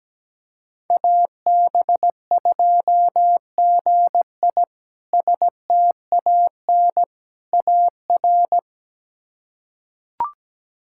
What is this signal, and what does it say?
Morse code, heard as a keyed tone, about 17 wpm, 700 Hz. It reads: AB2GI STAN AR
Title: K1USN SST QSO Practice - 17wpm